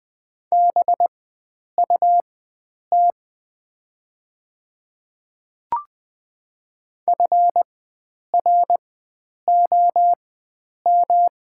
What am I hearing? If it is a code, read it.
Morse code, heard as a keyed tone, about 20 wpm, spaced as for 9 wpm, 700 Hz. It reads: BUT FROM
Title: 4x Character Spacing - Top 100 Words - 20wpm